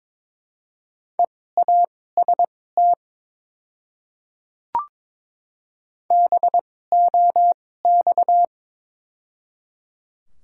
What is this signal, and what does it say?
Morse code, heard as a keyed tone, about 21 wpm, spaced as for 16 wpm, 700 Hz. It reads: EAST BOX
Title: 2x Character Spacing - Top 500 Words - 22wpm